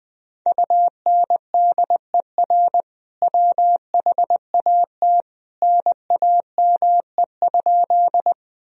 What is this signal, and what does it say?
Morse code, heard as a keyed tone, about 20 wpm, 700 Hz. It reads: UNDER WHAT NAME?
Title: Sentences from Top 1000 Words - Review Part 1 - 20wpm